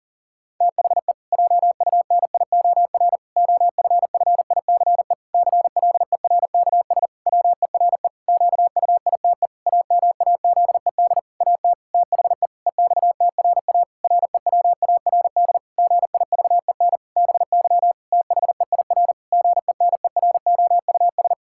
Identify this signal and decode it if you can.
Morse code, heard as a keyed tone, about 40 wpm, 700 Hz. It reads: THE JUNIOR OFFICE CLERKS WERE QUITE AMAZED AT THE EXTRA REWARD GIVEN BY THEIR GENEROUS
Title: Pangram Challenge v2 - 40wpm